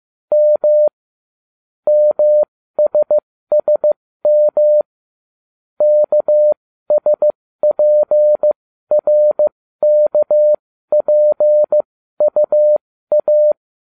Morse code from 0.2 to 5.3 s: M MSSM

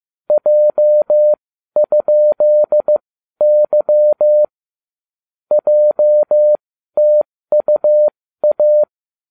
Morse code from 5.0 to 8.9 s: JTUA

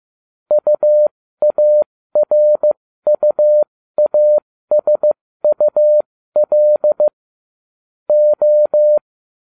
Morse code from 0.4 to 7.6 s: UARUASUL

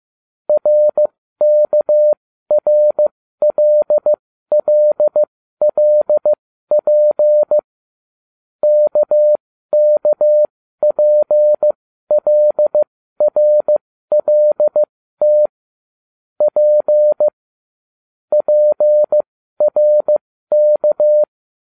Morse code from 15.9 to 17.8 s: P

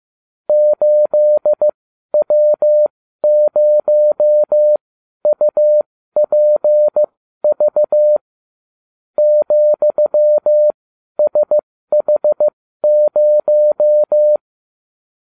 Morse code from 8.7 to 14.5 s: ,SH0

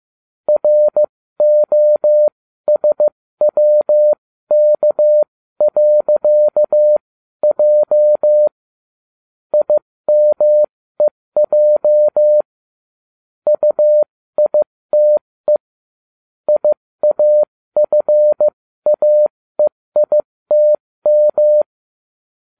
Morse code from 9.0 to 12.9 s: IMEJ

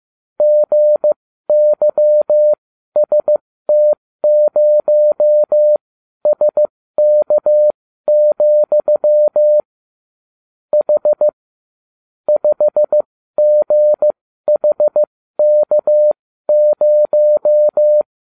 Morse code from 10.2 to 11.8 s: H

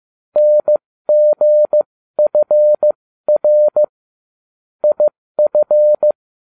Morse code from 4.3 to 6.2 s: IF

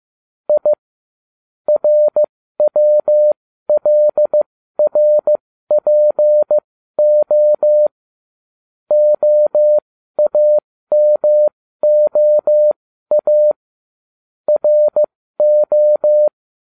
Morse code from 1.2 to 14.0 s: RWLRPO OAMOA

